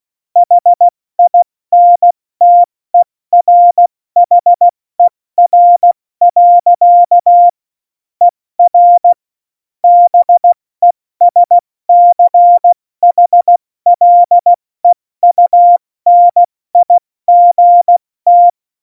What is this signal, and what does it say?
Morse code, heard as a keyed tone, about 16 wpm, spaced as for 14 wpm, 700 Hz. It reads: HINTERHER. ER BESCHLEUNIGT